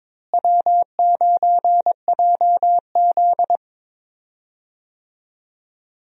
Morse code, heard as a keyed tone, about 22 wpm, 700 Hz. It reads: W9JZ
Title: USA Call Signs - 4 Characters - Repeated 2x - 22wpm